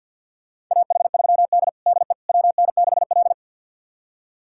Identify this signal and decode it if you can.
Morse code, heard as a keyed tone, about 52 wpm, 700 Hz. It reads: AH3D DE WN6L
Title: USA Callsign DE Callsign - 4 Characters - 50wpm